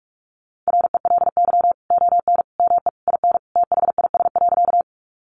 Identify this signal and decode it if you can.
Morse code, heard as a keyed tone, about 44 wpm, 700 Hz. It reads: RELY ON ME IN THIS.
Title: Sentences from Top 1200 Words 45wpm